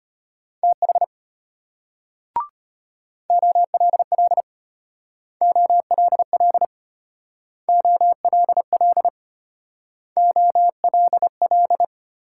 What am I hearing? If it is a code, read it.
Morse code, heard as a keyed tone, about 33 wpm, 700 Hz. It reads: TH OLL OLL OLL OLL